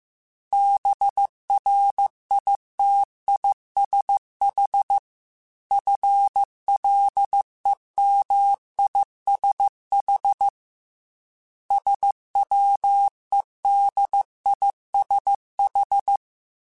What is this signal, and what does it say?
Morse code, heard as a keyed tone, about 15 wpm, 800 Hz. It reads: BRITISH FLEMISH SWEDISH
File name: General / General-mp3s / ISH @13